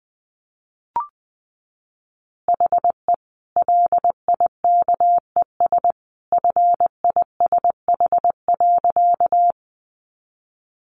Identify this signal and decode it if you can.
Morse code, heard as a keyed tone, about 20 wpm, 700 Hz. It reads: HE LIKES FISH.
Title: Easy Phrases 20wpm